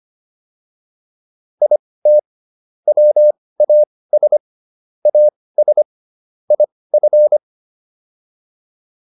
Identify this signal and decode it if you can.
Morse code, heard as a keyed tone, about 25 wpm, spaced as for 18 wpm, 600 Hz. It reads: IT WAS AS IF